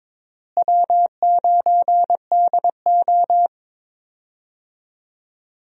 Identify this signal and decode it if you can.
Morse code, heard as a keyed tone, about 22 wpm, 700 Hz. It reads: W9DO
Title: USA Call Signs - 4 Characters - Repeated 3x - 22wpm